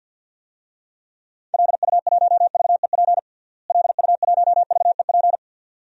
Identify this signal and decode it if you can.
Morse code, heard as a keyed tone, about 52 wpm, 700 Hz. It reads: LU1VEP LU1VEP
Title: DXCC-100 Call Signs - 6 Characters - Repeated 2x - 50wpm